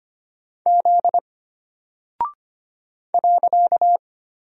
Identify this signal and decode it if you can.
Morse code, heard as a keyed tone, about 25 wpm, 700 Hz. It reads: Z .
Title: Single Character (Letter:Number:Punctuation:Prosign) Alt 25wpm